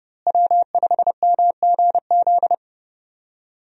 Morse code, heard as a keyed tone, about 30 wpm, 700 Hz. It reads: W5MGZ
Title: USA Call Signs - 5 Characters - Repeated 2x - 30wpm